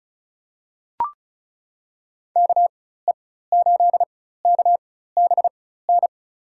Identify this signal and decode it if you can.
Morse code, heard as a keyed tone, about 36 wpm, spaced as for 16 wpm, 700 Hz. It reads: KE8KBN
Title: USA Call Signs - 4x Character Spacing - 35wpm